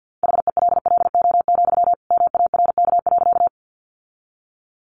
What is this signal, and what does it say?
Morse code, heard as a keyed tone, about 50 wpm, 700 Hz. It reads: HELLO, MARK.
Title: Sentences from Top 1100 Words 50wpm